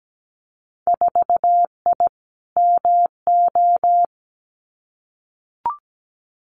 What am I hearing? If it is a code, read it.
Morse code, heard as a keyed tone, about 17 wpm, 700 Hz. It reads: 4I MO
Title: Winter Field Day Exchanges 17wpm